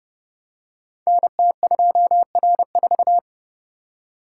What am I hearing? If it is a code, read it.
Morse code, heard as a keyed tone, about 29 wpm, 700 Hz. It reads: NT2R4
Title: Code Groups - 5 Letter:Numbers - 30wpm